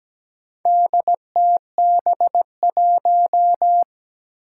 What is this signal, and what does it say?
Morse code, heard as a keyed tone, about 17 wpm, 700 Hz. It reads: DTB1
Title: Four Challenging Letter-Number Combinations 17wpm